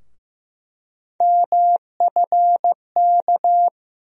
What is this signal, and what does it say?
Morse code, heard as a keyed tone, about 15 wpm, 700 Hz. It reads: MFK
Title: USA Call Sign Postfixes - 3 Letters 15wpm